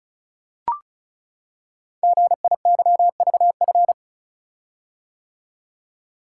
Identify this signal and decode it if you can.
Morse code, heard as a keyed tone, about 36 wpm, 700 Hz. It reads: GIYVF
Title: Code Groups - 5 Letters - 35wpm